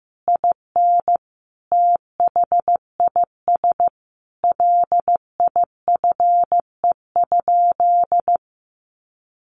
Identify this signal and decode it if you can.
Morse code, heard as a keyed tone, about 14 wpm, 700 Hz. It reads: IN THIS LIFE?
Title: Sentences from Top 1200 Words 15wpm